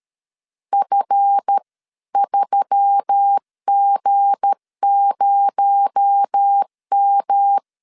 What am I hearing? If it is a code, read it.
Morse code, heard as a keyed tone, about 13 wpm, 800 Hz. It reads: F3G0M
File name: CW / DET / 1 9 1500C S A 12wpm 3800